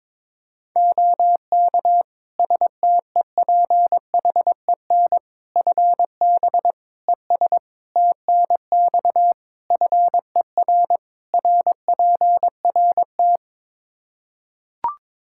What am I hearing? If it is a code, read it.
Morse code, heard as a keyed tone, about 22 wpm, 700 Hz. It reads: OK STEPHEN FB ES TNX FER RPRT
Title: K1USN SST QSO Practice 22wpm